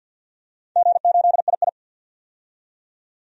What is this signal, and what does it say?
Morse code, heard as a keyed tone, about 46 wpm, 700 Hz. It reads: G8II
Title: DXCC-100 Call Signs - 4 Characters - 50wpm